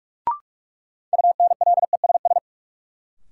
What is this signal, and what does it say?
Morse code, heard as a keyed tone, about 44 wpm, 700 Hz. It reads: UNLESS
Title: Sentences from Top 1000 Words - Review Part 5 - New Word Drill - 45wpm